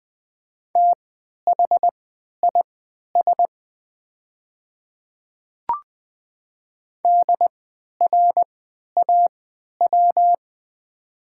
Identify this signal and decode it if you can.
Morse code, heard as a keyed tone, about 20 wpm, spaced as for 11 wpm, 690 Hz. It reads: THIS DRAW
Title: 3x Character Spacing - Top 300 Words - 20wpm